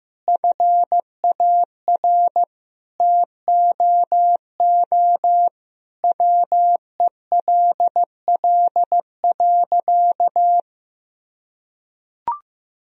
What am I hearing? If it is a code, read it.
Morse code, heard as a keyed tone, about 15 wpm, 700 Hz. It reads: FAR TOO WELL.
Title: Sentences from Top 1000 Words - Review Part 3 - 15wpm